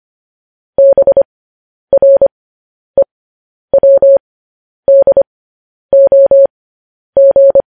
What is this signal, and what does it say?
Morse code, heard as a keyed tone, about 25 wpm, spaced as for 10 wpm, 550 Hz. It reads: BREWDOG